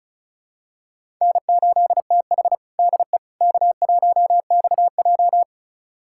Morse code, heard as a keyed tone, about 36 wpm, 700 Hz. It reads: N8TH DE K1XJ
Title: USA Callsign DE Callsign - 4 Characters - 35wpm